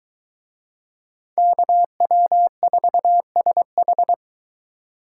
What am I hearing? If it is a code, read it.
Morse code, heard as a keyed tone, about 23 wpm, 700 Hz. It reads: KW4SH